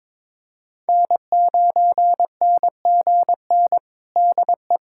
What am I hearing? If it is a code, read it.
Morse code, heard as a keyed tone, about 22 wpm, 700 Hz. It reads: N9NGN DE
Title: USA Callsign DE Callsign 22wpm